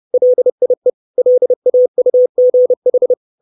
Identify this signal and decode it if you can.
Morse code, heard as a keyed tone, about 30 wpm, 500 Hz. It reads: LIE LAUGH
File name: 632 Words 30WPM